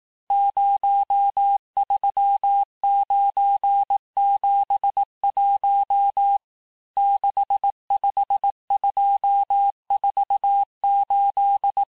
Morse code, heard as a keyed tone, about 18 wpm, 800 Hz. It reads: 03971 65248
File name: cours-du-lundi files / cours-1588-texte-chiffre-vitesse-17MPM